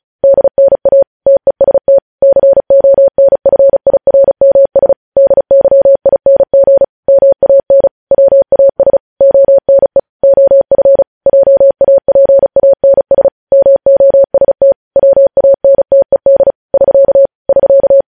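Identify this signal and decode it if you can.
Morse code, heard as a keyed tone, about 36 wpm, 550 Hz. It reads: DNA TEST CONFIRMS DYING MAN WAS ONE OF JAPANS MOST WANTED <SK> <SK>
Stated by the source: Morse code news headlines at 35.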